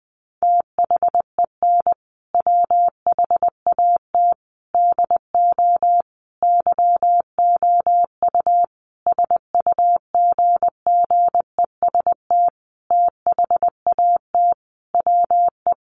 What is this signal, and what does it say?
Morse code, heard as a keyed tone, about 20 wpm, 700 Hz. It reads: THEN WHAT DO YOU SUGGEST THAT WE
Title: Sentences from Top 1300 Words 20wpm